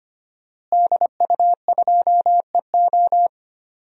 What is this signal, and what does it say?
Morse code, heard as a keyed tone, about 25 wpm, 700 Hz. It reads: DU2EO
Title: DXCC-100 Call Signs - 5 Characters - Repeated 3x - 25wpm